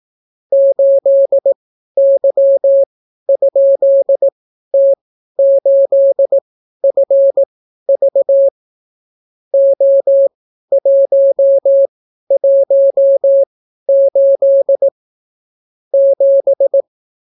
Morse code, heard as a keyed tone, about 18 wpm, spaced as for 12 wpm, 550 Hz. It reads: 8Y?T8FV O118 7